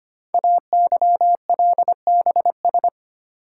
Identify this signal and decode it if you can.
Morse code, heard as a keyed tone, about 25 wpm, 700 Hz. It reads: AYLBS
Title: Code Groups - 5 Letter:Numbers - 25wpm